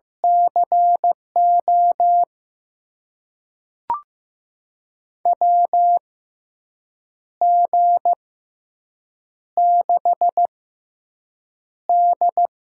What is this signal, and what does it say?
Morse code, heard as a keyed tone, about 15 wpm, spaced as for 6 wpm, 700 Hz. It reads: CO WG6D
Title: USA Call Signs - 6x Character Spacing - 15wpm